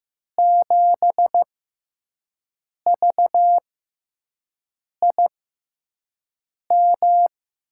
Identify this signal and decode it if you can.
Morse code, heard as a keyed tone, about 15 wpm, spaced as for 5 wpm, 700 Hz. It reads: 7VIM
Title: USA Call Signs - 6 Characters - 6x Character Spacing - 15wpm